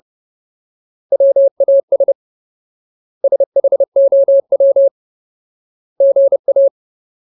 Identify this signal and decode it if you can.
Morse code, heard as a keyed tone, about 30 wpm, 550 Hz. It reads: WAS SHOW GA
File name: Ultimate Word List / 30WPM Ultimate